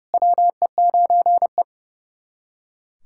Morse code, heard as a keyed tone, about 32 wpm, 700 Hz. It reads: WE9E
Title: SOTA QSO Practice 30wpm